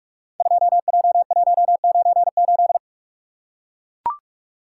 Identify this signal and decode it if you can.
Morse code, heard as a keyed tone, about 44 wpm, 700 Hz. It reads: 2J198
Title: Code Groups - 5 Letter:Numbers - 45wpm